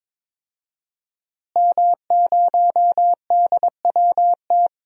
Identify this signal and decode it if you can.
Morse code, heard as a keyed tone, about 22 wpm, 700 Hz. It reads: M0DWT
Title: DXCC-100 Call Signs - 5 Characters - Repeated 3x - 22wpm